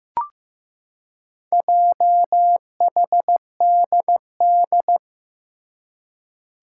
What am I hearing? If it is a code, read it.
Morse code, heard as a keyed tone, about 15 wpm, 700 Hz. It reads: JHDD
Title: Four Challenging Letter-Number Combinations 15wpm